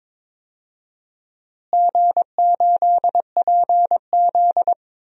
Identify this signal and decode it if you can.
Morse code, heard as a keyed tone, about 22 wpm, 700 Hz. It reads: G8PZ
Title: DXCC-100 Call Signs - 4 Characters - Repeated 2x - 22wpm